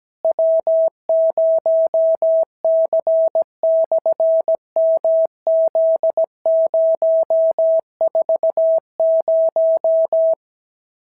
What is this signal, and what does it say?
Morse code, heard as a keyed tone, about 16 wpm, 650 Hz. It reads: W0C/MZ040
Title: SOTA QSO Practice 17wpm